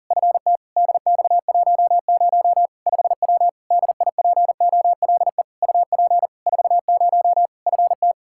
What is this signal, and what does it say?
Morse code, heard as a keyed tone, about 37 wpm, 700 Hz. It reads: FT DX10 5W DIPOLE UP 40 FT
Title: Practice QSO - Exchange 2 - Info - 40wpm